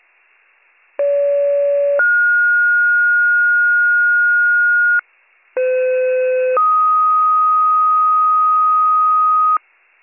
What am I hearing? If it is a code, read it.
Two-tone page, heard as a one-second tone, then a three-second tone, about 570 then 1430 Hz, about 510 then 1250 Hz, with noise. It is heard from 1.0 to 9.6 s.